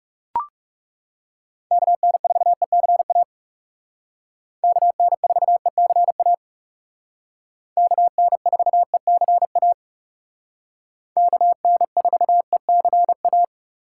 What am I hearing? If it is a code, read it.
Morse code, heard as a keyed tone, about 35 wpm, 700 Hz. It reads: KN4ECA KN4ECA KN4ECA KN4ECA
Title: Speed Racing - USA 2x3 Call Signs 30wpm